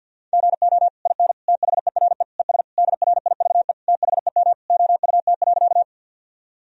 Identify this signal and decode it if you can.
Morse code, heard as a keyed tone, about 52 wpm, 700 Hz. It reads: GO IN THERE ES DRIVE THEM OUT.
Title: Sentences from Top 500 Words and Abbreviations 50wpm